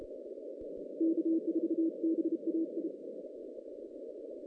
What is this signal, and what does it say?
Morse code, heard as a keyed tone, about 33 wpm, 330 Hz. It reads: K4BAI
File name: 2016 JAN CW NAQP / 2016 Jan naqp k5ot - 0-7